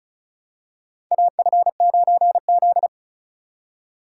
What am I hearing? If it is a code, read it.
Morse code, heard as a keyed tone, about 36 wpm, 700 Hz. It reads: AF9Z
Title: USA Call Signs - 4 Characters - 3x Character Spacing - 35wpm